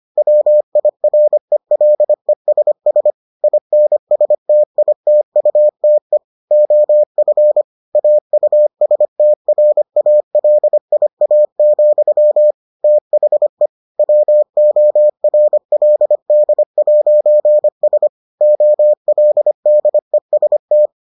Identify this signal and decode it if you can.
Morse code, heard as a keyed tone, about 25 wpm, 600 Hz. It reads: WIRELESS INSTITUTE OF AUSTRALIA, THE WORLD'S OLDEST